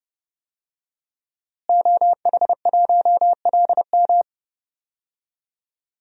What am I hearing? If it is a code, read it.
Morse code, heard as a keyed tone, about 30 wpm, 700 Hz. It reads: OH1LM
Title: DXCC-100 Call Signs - 5 Characters - Repeated 2x - 30wpm